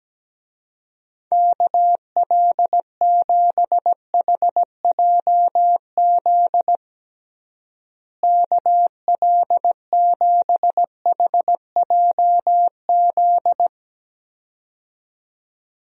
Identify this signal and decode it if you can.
Morse code, heard as a keyed tone, about 17 wpm, 700 Hz. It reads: KL7HJZ KL7HJZ
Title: DXCC-100 Call Signs - 6 Characters - Repeated 2x - 17wpm